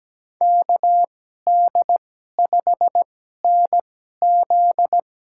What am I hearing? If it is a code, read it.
Morse code, heard as a keyed tone, about 17 wpm, spaced as for 12 wpm, 700 Hz. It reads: KD5NZ
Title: USA Call Signs - 2x Character Spacing - 17wpm